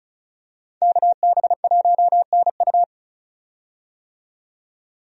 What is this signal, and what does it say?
Morse code, heard as a keyed tone, about 36 wpm, 700 Hz. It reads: KB1NU
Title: USA Call Signs - 5 Characters - 35wpm